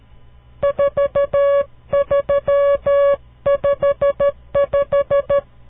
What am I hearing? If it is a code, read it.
Morse code, heard as a keyed tone, about 13 wpm, 560 Hz. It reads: 4355
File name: CW Learning Programs / farnsworth / unit 5